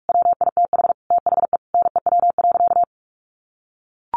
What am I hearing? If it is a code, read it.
Morse code, heard as a keyed tone, about 44 wpm, 700 Hz. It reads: WITH THE NEW.
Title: Sentences from Top 1300 Words 45wpm